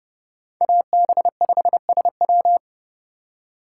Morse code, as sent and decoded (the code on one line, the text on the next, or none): .- -... ..... ... .--
AB5SW